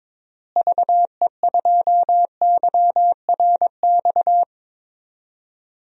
Morse code, heard as a keyed tone, about 22 wpm, 700 Hz. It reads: VE2YRX
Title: DXCC-100 Call Signs - 6 Characters - Repeated 3x - 22wpm